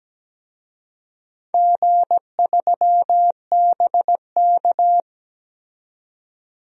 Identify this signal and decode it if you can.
Morse code, heard as a keyed tone, about 17 wpm, 700 Hz. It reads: G3BK
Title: DXCC-100 Call Signs - 4 Characters - 17wpm